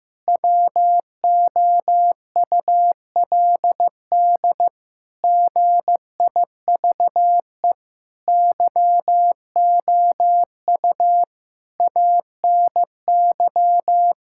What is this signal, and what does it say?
Morse code, heard as a keyed tone, about 15 wpm, 700 Hz. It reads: WOULD GIVE YOU ANY
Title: Sentences from Top 1600 Words 15wpm